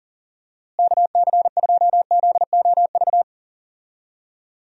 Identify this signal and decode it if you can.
Morse code, heard as a keyed tone, about 40 wpm, 700 Hz. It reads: KC2ZOV
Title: USA Call Signs - 6 Characters - Repeated 2x - 40wpm